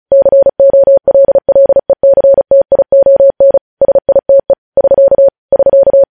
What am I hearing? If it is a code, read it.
Morse code, heard as a keyed tone, about 36 wpm, 550 Hz. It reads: COLLECTION SITE <SK> <SK>
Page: news headlines in morse code at 35 wpm